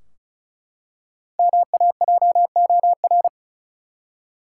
Morse code, heard as a keyed tone, about 36 wpm, 700 Hz. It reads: MAJOR
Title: Five Letter Words - 35wpm